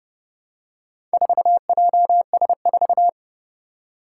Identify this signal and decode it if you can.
Morse code, heard as a keyed tone, about 30 wpm, 700 Hz. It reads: 4JS4
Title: Four Challenging Letter-Number Combinations 30wpm